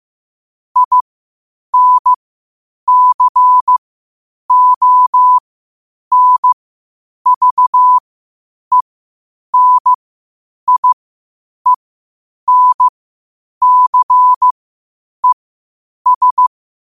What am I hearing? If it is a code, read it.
Morse code, heard as a keyed tone, about 15 wpm, spaced as for 8 wpm, 1000 Hz. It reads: INCONVENIENCES
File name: qotdfiles / 8wpm